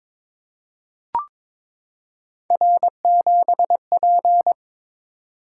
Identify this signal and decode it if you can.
Morse code, heard as a keyed tone, about 23 wpm, 700 Hz. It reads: R7P
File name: practice / 22wpm / Three Letter-Number Combinations - 22wpm